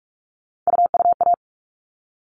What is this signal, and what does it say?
Morse code, heard as a keyed tone, about 44 wpm, 700 Hz. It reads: UUA
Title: All Cut Numbers - 3-Digit 45wpm